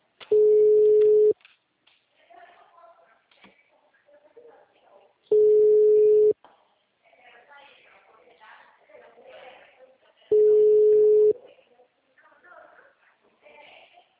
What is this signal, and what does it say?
Ringback tone (Europe, 425 Hz), with speech.